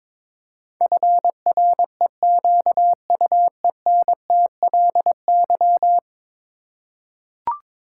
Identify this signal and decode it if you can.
Morse code, heard as a keyed tone, about 22 wpm, 700 Hz. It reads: FREQUENTLY
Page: morse-code-ninja-practice-22wpm